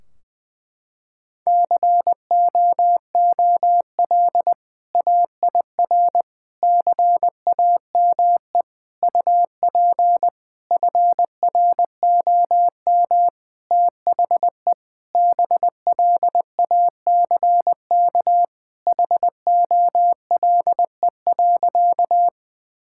Morse code, heard as a keyed tone, about 20 wpm, 700 Hz. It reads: COOL AIR CAME UP FROM THE BLACK HOLE.
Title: Sentences from Top 1600 Words 20wpm